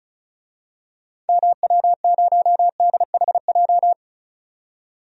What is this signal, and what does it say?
Morse code, heard as a keyed tone, about 36 wpm, 700 Hz. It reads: MW0DHJ
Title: DXCC-100 Call Signs - 6 Characters - 35wpm